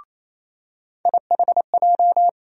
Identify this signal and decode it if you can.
Morse code, heard as a keyed tone, about 28 wpm, 700 Hz. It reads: IHJ
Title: USA Call Sign Postfixes - 3 Letters 28wpm